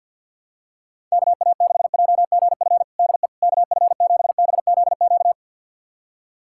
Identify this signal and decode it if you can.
Morse code, heard as a keyed tone, about 51 wpm, 700 Hz. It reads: KA6JGF DE KF7BZQ
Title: USA Callsign DE Callsign - 6 Characters - 50wpm